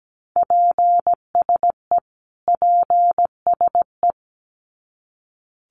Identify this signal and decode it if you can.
Morse code, heard as a keyed tone, about 17 wpm, 700 Hz. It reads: PSE PSE
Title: Abbreviations Repeated 3x - 17wpm